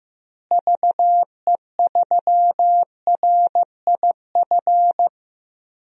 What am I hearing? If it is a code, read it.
Morse code, heard as a keyed tone, about 15 wpm, 700 Hz. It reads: VE3RIF
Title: DXCC-100 Call Signs - 6 Characters - Repeated 3x - 15wpm